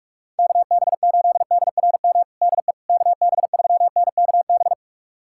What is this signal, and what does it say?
Morse code, heard as a keyed tone, about 41 wpm, 700 Hz. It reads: KB8DRM DE KB3NKB
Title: USA Callsign DE Callsign - 6 Characters - 45wpm